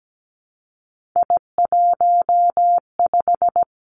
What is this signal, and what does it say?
Morse code, heard as a keyed tone, about 17 wpm, 700 Hz. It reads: I15
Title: Three Challenging Letter-Number Combinations 17wpm